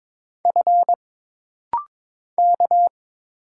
Morse code, heard as a keyed tone, about 22 wpm, 700 Hz. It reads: F K
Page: This rapid-fire timing is geared to push your ICR (Instant Character Recognition) skills.